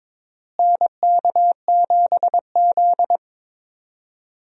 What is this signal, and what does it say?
Morse code, heard as a keyed tone, about 22 wpm, 700 Hz. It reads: NK7Z
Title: USA Call Signs - 4 Characters - Repeated 3x - 22wpm